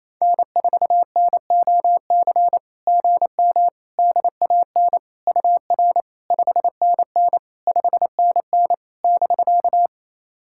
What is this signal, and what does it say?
Morse code, heard as a keyed tone, about 28 wpm, 700 Hz. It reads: N4NOC GM DAN UR 5NN 5NN <BK>
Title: SOTA QSO Practice 28wpm